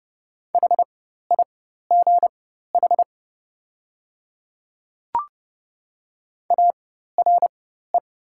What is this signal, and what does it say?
Morse code, heard as a keyed tone, about 30 wpm, spaced as for 14 wpm, 700 Hz. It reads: HIGH ARE
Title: 4x Character Spacing - Top 200 Words - 30wpm